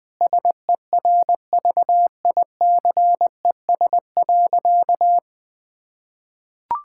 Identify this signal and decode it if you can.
Morse code, heard as a keyed tone, about 20 wpm, 700 Hz. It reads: SERVICES.
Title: Sentences with 4-Word Phrases - 20wpm